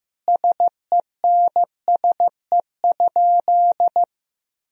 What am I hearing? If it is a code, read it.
Morse code, heard as a keyed tone, about 15 wpm, 700 Hz. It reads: SENSE?
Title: Sentences from Top 1000 Words - Review Part 4 - 15wpm